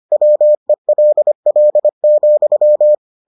Morse code, heard as a keyed tone, about 25 wpm, 600 Hz. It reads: WELL,